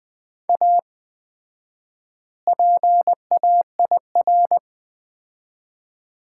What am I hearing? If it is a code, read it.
Morse code, heard as a keyed tone, about 21 wpm, 700 Hz. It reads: A PAIR
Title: 4x Word-Spacing - Sets of 3 Words - 20wpm